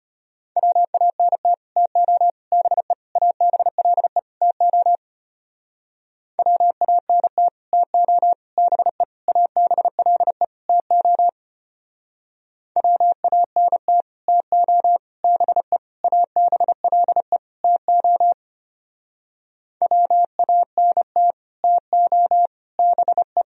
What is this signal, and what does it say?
Morse code, heard as a keyed tone, about 32 wpm, 700 Hz. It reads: WANT TO BE ABLE TO WANT TO BE ABLE TO WANT TO BE ABLE TO WANT TO BE